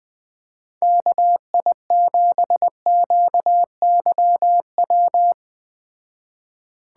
Morse code, heard as a keyed tone, about 19 wpm, 700 Hz. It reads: KI7QYW
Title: Speed Racing - USA 2x3 Call Signs 17wpm